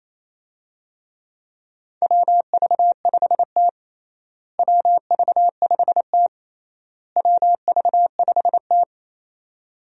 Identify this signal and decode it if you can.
Morse code, heard as a keyed tone, about 28 wpm, 700 Hz. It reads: WV5T WV5T WV5T
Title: USA Call Signs - 4 Characters - Repeated 3x - 28wpm